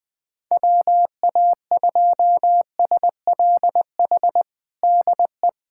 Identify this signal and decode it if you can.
Morse code, heard as a keyed tone, about 20 wpm, 700 Hz. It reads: WA2SLH DE
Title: POTA QSO Practice - 20wpm